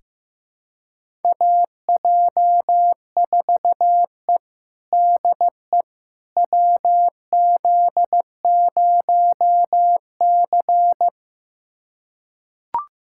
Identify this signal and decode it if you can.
Morse code, heard as a keyed tone, about 15 wpm, 700 Hz. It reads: AJ4E DE WZ0C
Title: USA Callsign DE Callsign - 4 Characters - 15wpm